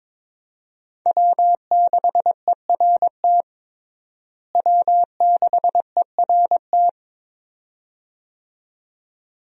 Morse code, heard as a keyed tone, about 21 wpm, 700 Hz. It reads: W6ERT W6ERT
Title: USA Call Signs - 5 Characters - Repeated 2x - 22wpm